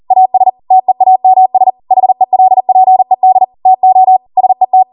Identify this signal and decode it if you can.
Morse code, heard as a keyed tone, about 40 wpm, 750 Hz. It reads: AS TEAMS HELPED TO SET